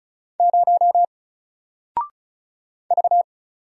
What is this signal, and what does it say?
Morse code, heard as a keyed tone, about 36 wpm, 700 Hz. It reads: 0 V
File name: Single-Letter-Number-35wpm